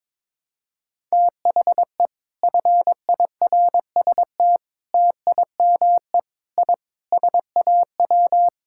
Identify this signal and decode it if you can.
Morse code, heard as a keyed tone, about 23 wpm, 700 Hz. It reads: THE FIRST TIME I SAW
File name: practice / 22wpm / Sentences with 4-Word Phrases - 22wpm